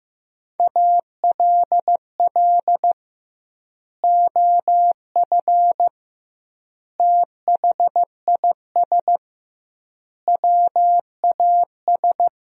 Morse code, heard as a keyed tone, about 15 wpm, 700 Hz. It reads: ALL OF THIS WAS
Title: 2x Word-Spacing - Sentences from Top 500 Words - 15wpm